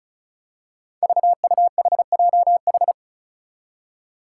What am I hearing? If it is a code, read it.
Morse code, heard as a keyed tone, about 35 wpm, 700 Hz. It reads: VUHJH
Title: Five Challenging Letter-Number Combinations 35wpm